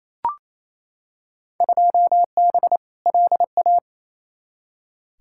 Morse code, heard as a keyed tone, about 28 wpm, 700 Hz. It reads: 2B LA
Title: Field Day Exchanges Encore - 28wpm